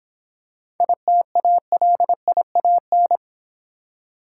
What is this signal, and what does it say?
Morse code, heard as a keyed tone, about 26 wpm, 700 Hz. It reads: ITALIAN